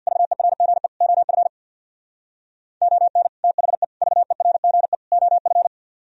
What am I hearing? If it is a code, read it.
Morse code, heard as a keyed tone, about 64 wpm, 700 Hz. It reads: VERGE OF ON THE VERGE OF